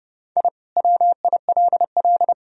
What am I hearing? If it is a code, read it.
Morse code, heard as a keyed tone, about 30 wpm, 700 Hz. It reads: I WILL
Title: Sentences from Top 1000 Words - Review Part 1 - 30wpm